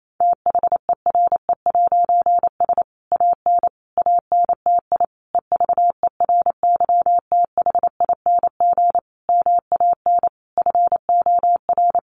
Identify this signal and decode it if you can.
Morse code, heard as a keyed tone, about 28 wpm, 700 Hz. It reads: THERE'S AN ANTI EVERYTHING MAN FOR
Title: A Sound Of Thunder 28wpm